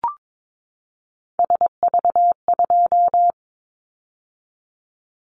Morse code, heard as a keyed tone, about 22 wpm, 700 Hz. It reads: SV2
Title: Callsign Prefixes for Top 50 DXCC - 22wpm